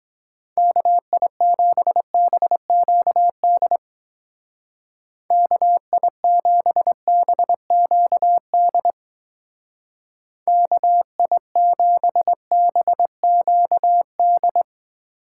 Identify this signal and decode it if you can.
Morse code, heard as a keyed tone, about 21 wpm, 700 Hz. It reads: KI7BQD KI7BQD KI7BQD
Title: Speed Racing - USA 2x3 Call Signs 17wpm